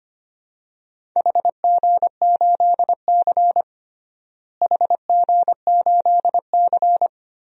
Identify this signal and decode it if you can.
Morse code, heard as a keyed tone, about 25 wpm, 700 Hz. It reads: HG8C HG8C
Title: DXCC-100 Call Signs - 4 Characters - Repeated 3x - 25wpm